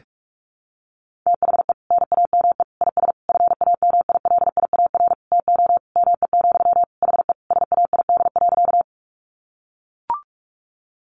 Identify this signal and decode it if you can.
Morse code, heard as a keyed tone, about 44 wpm, 700 Hz. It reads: THE NAME IS FAMILIAR TO ME, HE SAID.
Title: Sentences from Top 1000 Words 45wpm